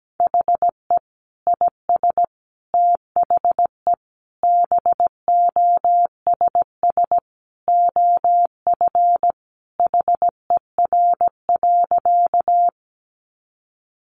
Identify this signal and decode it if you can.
Morse code, heard as a keyed tone, about 17 wpm, 700 Hz. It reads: HE IS THE BOSS OF HER.
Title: Sentences from Top 700 Words 17wpm